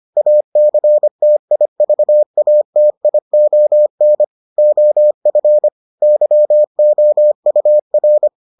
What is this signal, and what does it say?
Morse code, heard as a keyed tone, about 25 wpm, 600 Hz. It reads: ACTIVATION OF YOUR